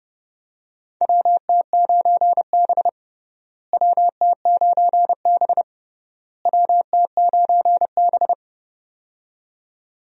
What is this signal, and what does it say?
Morse code, heard as a keyed tone, about 30 wpm, 700 Hz. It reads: WT9B WT9B WT9B
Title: USA Call Signs - 4 Characters - Repeated 3x - 30wpm